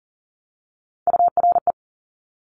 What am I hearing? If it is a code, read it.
Morse code, heard as a keyed tone, about 43 wpm, 700 Hz. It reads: URE
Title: Top Three Letter Combinations in Words 40wpm